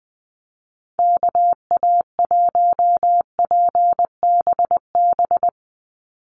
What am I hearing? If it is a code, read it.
Morse code, heard as a keyed tone, about 20 wpm, 700 Hz. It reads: KA1PBB
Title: USA Call Signs 20wpm